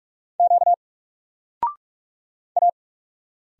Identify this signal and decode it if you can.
Morse code, heard as a keyed tone, about 44 wpm, 700 Hz. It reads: Q A
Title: Single Letter-Number 45wpm